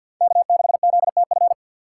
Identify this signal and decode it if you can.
Morse code, heard as a keyed tone, about 53 wpm, 700 Hz. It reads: K6ZTF
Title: USA Call Signs - 5 Characters - 2x Character Spacing - 50wpm